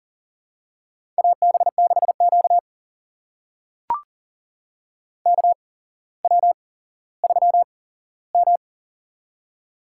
Morse code, heard as a keyed tone, about 38 wpm, spaced as for 17 wpm, 700 Hz. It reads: AB6Q KW3M
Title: USA Call Signs - 4 Characters - 8x Character Spacing - 40wpm